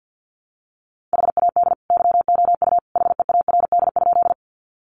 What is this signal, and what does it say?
Morse code, heard as a keyed tone, about 50 wpm, 700 Hz. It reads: HAD YOU HEARD?
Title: Sentences from Top 400 Words 50wpm